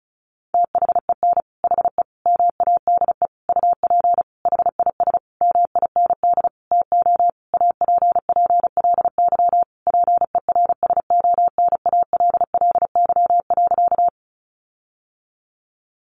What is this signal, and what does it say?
Morse code, heard as a keyed tone, about 34 wpm, 700 Hz. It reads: THEN HE MADE UP HIS MIND TO APPLY PERSONALLY.
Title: Sentences from Top 1400 Words 35wpm